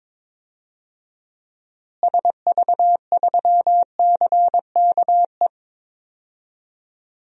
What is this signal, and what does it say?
Morse code, heard as a keyed tone, about 22 wpm, 700 Hz. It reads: SV3CKE
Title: DXCC-100 Call Signs - 6 Characters - Repeated 3x - 22wpm